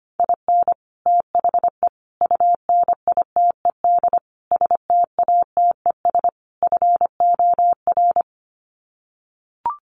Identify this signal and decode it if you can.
Morse code, heard as a keyed tone, about 25 wpm, 700 Hz. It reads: IN THE UNITED STATES FOR
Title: Sets of 5 Words Encore2 25wpm